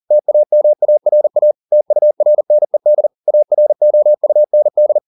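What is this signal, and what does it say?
Morse code, heard as a keyed tone, about 40 wpm, 600 Hz. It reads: TAMARA TURNED AROUND
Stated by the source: Morse code at 40 wpm, practice text, CW med Gnister.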